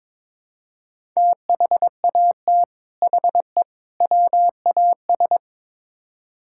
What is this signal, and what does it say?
Morse code, heard as a keyed tone, about 22 wpm, 700 Hz. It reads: THAT HE WAS
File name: practice / 22wpm / Build A Phrase - 5-Word Phrases - 22wpm